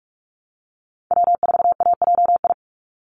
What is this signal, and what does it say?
Morse code, heard as a keyed tone, about 44 wpm, 700 Hz. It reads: W4AJI
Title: USA Call Signs 45wpm